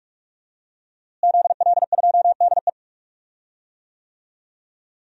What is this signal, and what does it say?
Morse code, heard as a keyed tone, about 44 wpm, 700 Hz. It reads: ZL2DE
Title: DXCC-100 Call Signs - 5 Characters - Repeated 3x - 45wpm